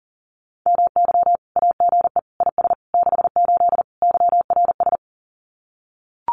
Morse code, heard as a keyed tone, about 40 wpm, 700 Hz. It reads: MY AGE IS 68 YRS
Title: Common QSO Phrases Encore 40wpm